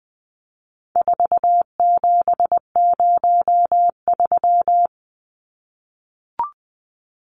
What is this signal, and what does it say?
Morse code, heard as a keyed tone, about 20 wpm, 700 Hz. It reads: 4703
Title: Numbers - 4-Digit 20wpm